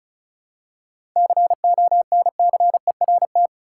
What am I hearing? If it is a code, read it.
Morse code, heard as a keyed tone, about 36 wpm, 700 Hz. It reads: CONCERT
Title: Seven Letter Words - 35wpm